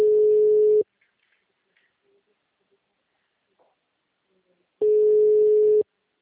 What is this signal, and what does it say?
Ringback tone (Europe, 425 Hz).
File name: out- -2036- - - .17050